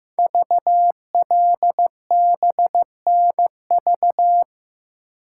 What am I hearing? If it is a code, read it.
Morse code, heard as a keyed tone, about 15 wpm, 700 Hz. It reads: VLBNV
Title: Code Groups - 5 Letter:Numbers - 15wpm